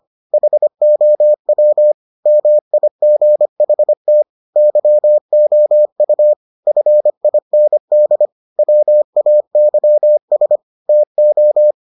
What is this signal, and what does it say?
Morse code, heard as a keyed tone, about 25 wpm, 600 Hz. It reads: HOW MIGHT YOU FIND WAYS TO